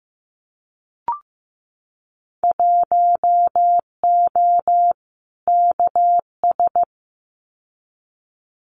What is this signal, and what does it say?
Morse code, heard as a keyed tone, about 15 wpm, 700 Hz. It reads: 1O KS
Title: Winter Field Day Exchanges 15wpm